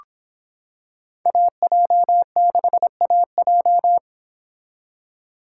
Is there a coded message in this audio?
Yes: Morse code from 1.3 to 4.0 s, heard as a keyed tone, about 26 wpm, 700 Hz. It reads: AJ6AJ